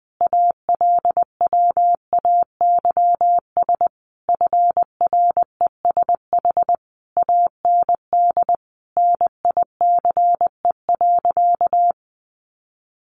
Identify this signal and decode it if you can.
Morse code, heard as a keyed tone, about 20 wpm, 700 Hz. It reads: ALWAYS FRESH AND NICE.
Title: Sentences from Top 1200 Words 20wpm